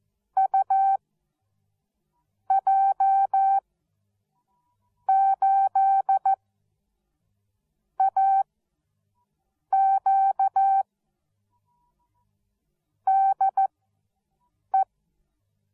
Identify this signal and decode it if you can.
Morse code, heard as a keyed tone, about 14 wpm, spaced as for 5 wpm, 780 Hz. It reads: UJ8AQ DE